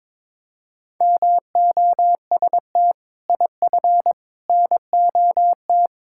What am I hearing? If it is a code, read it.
Morse code, heard as a keyed tone, about 22 wpm, 700 Hz. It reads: MOST IF NOT
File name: practice / 22wpm / Sets of 6 Words - 22wpm